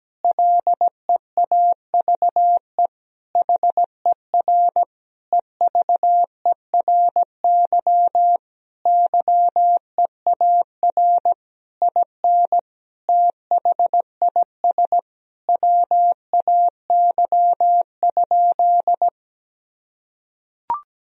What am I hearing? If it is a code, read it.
Morse code, heard as a keyed tone, about 17 wpm, 700 Hz. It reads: LEAVE HER EVERY YEAR IN THIS WAY?
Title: Sentences from Top 1000 Words - Review Part 1 - 17wpm